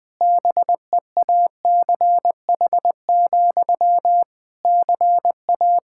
Morse code, heard as a keyed tone, about 20 wpm, 700 Hz. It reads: BEACH, CA
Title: Practice QSO - Exchange 1 - Beginning - 20wpm